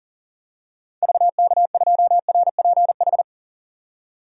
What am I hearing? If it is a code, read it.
Morse code, heard as a keyed tone, about 40 wpm, 700 Hz. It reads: VK2RPH
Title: DXCC-100 Call Signs - 6 Characters - 40wpm